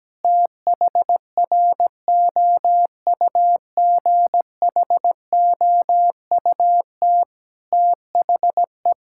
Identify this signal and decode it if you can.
Morse code, heard as a keyed tone, about 17 wpm, 700 Hz. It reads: THROUGHOUT THE
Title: Sentences from Top 1000 Words - Review Part 5 - 17wpm